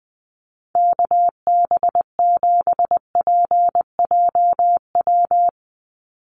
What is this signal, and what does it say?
Morse code, heard as a keyed tone, about 20 wpm, 700 Hz. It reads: KB7PJW
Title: USA Call Signs 20wpm